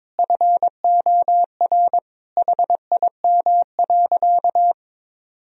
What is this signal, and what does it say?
Morse code, heard as a keyed tone, about 22 wpm, 700 Hz. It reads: FOR HIM.
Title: Sentences from Top 1000 Words - Review Part 4 - 22wpm